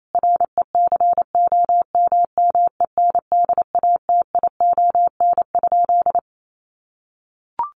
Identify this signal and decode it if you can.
Morse code, heard as a keyed tone, about 28 wpm, 700 Hz. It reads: RECOMMENDATION?
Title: Sentences from Top 1400 Words 28wpm